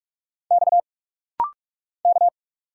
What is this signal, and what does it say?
Morse code, heard as a keyed tone, about 45 wpm, 700 Hz. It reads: X K
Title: Single Letter-Number - Rapid-Fire 45wpm